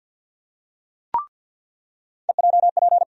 Morse code, heard as a keyed tone, about 47 wpm, 700 Hz. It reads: EJP
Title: USA Call Sign Postfixes - 3 Letters 50wpm